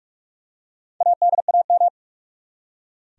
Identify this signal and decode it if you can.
Morse code, heard as a keyed tone, about 46 wpm, 700 Hz. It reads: ADAM
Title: USA Common Names - Max 4 Letters - 45wpm